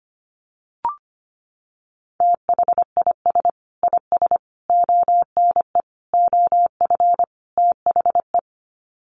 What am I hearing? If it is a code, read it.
Morse code, heard as a keyed tone, about 25 wpm, 700 Hz. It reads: THIS IS ONE OF THE
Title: Sets of 5 Words 25wpm